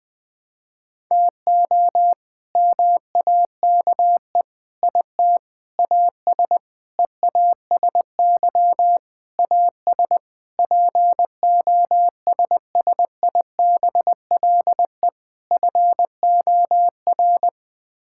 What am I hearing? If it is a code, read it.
Morse code, heard as a keyed tone, about 21 wpm, 700 Hz. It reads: TO MAKE IT AS EASY AS POSSIBLE FOR
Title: Sets of 8 Words - 20wpm